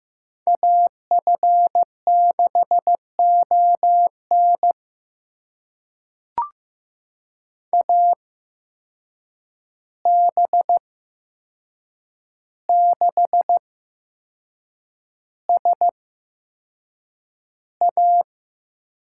Morse code, heard as a keyed tone, about 15 wpm, spaced as for 6 wpm, 700 Hz. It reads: AF6ON AB6SA
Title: USA Call Signs - 5 Characters - 8x Character Spacing - 15wpm